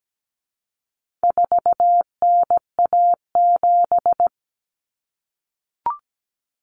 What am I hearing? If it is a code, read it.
Morse code, heard as a keyed tone, about 17 wpm, 700 Hz. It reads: 4NA7
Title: Common Cut Numbers - 4-Digit 17wpm